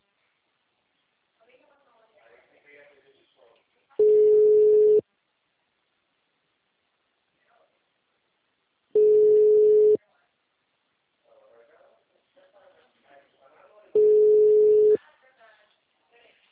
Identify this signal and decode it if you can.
Ringback tone (Europe, 425 Hz), with speech.
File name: out- -2019- - - .26009